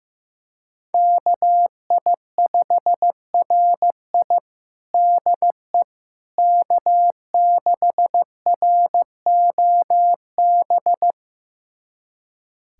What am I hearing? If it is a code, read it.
Morse code, heard as a keyed tone, about 15 wpm, 700 Hz. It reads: KI5RI DE K6ROB
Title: USA Callsign DE Callsign - 5 Characters - 15wpm